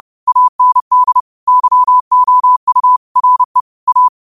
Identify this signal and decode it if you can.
Morse code, heard as a keyed tone, about 30 wpm, 1000 Hz. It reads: AND YOU RE A
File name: qotdfiles / 30wpm